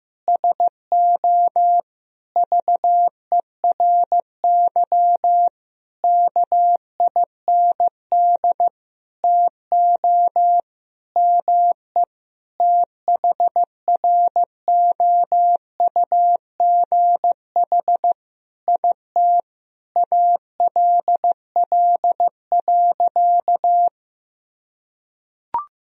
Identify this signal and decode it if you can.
Morse code, heard as a keyed tone, about 15 wpm, 700 Hz. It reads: SO VERY KIND TO ME THROUGH IT ALL.
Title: Sentences from Top 1000 Words - Review Part 1 - 15wpm